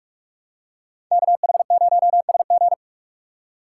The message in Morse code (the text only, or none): KH0SG